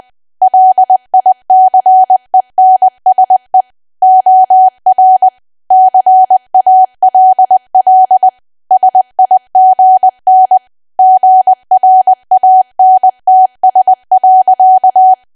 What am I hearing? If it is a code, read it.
Morse code, heard as a keyed tone, about 20 wpm, 750 Hz. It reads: LICENSE OR CALL SIGN GRANTS.
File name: cwa / W1AW-Level3 / 20WPM